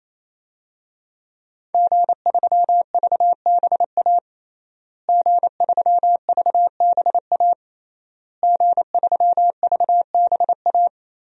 Morse code, heard as a keyed tone, about 28 wpm, 700 Hz. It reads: G3VBA G3VBA G3VBA
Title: DXCC-100 Call Signs - 5 Characters - Repeated 3x - 28wpm